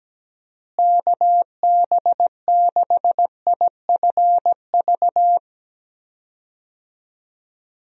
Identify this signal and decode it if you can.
Morse code, heard as a keyed tone, about 17 wpm, 700 Hz. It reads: KB6IFV
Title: USA Call Signs - 6 Characters - Repeated 3x - 17wpm